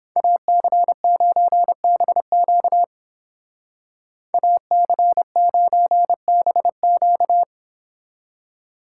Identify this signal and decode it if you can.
Morse code, heard as a keyed tone, about 29 wpm, 700 Hz. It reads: AC9BQ AC9BQ